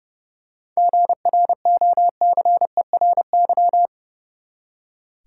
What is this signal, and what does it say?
Morse code, heard as a keyed tone, about 32 wpm, 700 Hz. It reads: GROCERY
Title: Seven Letter Words - 30wpm